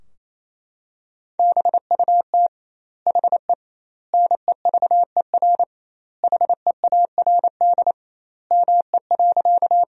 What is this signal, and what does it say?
Morse code, heard as a keyed tone, about 28 wpm, 700 Hz. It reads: BUT HE NEVER HEARD ME.
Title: 2x Word-Spacing - Sentences from Top 400 Words - 28wpm